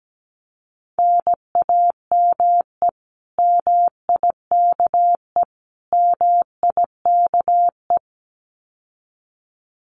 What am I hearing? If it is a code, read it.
Morse code, heard as a keyed tone, about 17 wpm, 700 Hz. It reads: NAME MIKE MIKE
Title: Common QSO Phrases Encore 17wpm